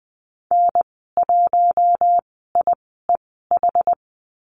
Morse code, heard as a keyed tone, about 20 wpm, spaced as for 15 wpm, 700 Hz. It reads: N1IEH
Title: 082 Lesson 14g - Number 2 - Call Signs 20wpm